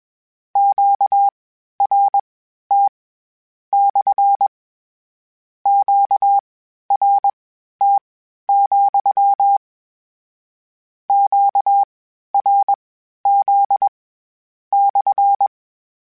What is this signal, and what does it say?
Morse code, heard as a keyed tone, about 21 wpm, spaced as for 12 wpm, 800 Hz. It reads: QRT / QRT, QRZ /
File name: Q Calls / Q Calls-mp3s / Q CALLS SPSL @20